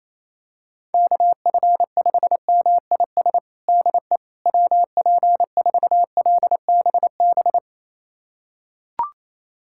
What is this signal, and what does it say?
Morse code, heard as a keyed tone, about 28 wpm, 700 Hz. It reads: KF5MIS DE WP4LBB
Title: USA Callsign DE Callsign - 6 Characters - 28wpm